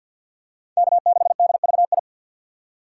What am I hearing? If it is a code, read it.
Morse code, heard as a keyed tone, about 51 wpm, 700 Hz. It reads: K6DVI
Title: USA Call Signs - 5 Characters - 2x Character Spacing - 50wpm